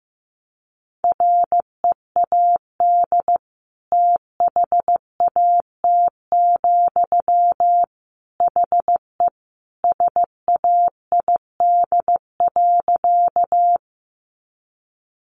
Morse code, heard as a keyed tone, about 15 wpm, 700 Hz. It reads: READ THAT, HE SAID.
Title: Sentences from Top 200 Words 15wpm